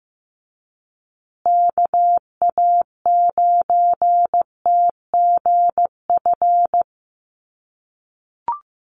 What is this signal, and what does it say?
Morse code, heard as a keyed tone, about 15 wpm, 700 Hz. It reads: KA9TGF
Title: USA Call Signs Repeated 2x 15wpm